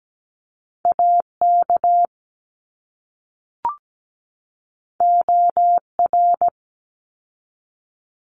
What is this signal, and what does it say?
Morse code, heard as a keyed tone, about 17 wpm, 700 Hz. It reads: AK OR
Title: ARRL Sections 17wpm